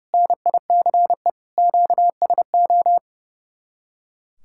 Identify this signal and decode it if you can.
Morse code, heard as a keyed tone, about 36 wpm, 700 Hz. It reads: NICE QSO
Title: Speed-Racing Common QSO Phrases - 30wpm